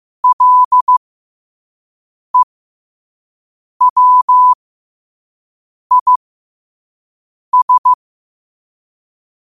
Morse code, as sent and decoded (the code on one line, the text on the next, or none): .-.. .-- .. ...
LEWIS